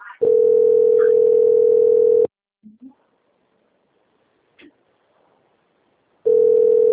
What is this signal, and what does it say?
Ringback tone (North America), 440 + 480 Hz.